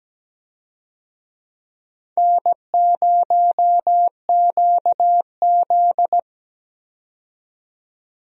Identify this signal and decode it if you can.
Morse code, heard as a keyed tone, about 17 wpm, 700 Hz. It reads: N0QZ
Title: USA Call Signs - 4 Characters - 17wpm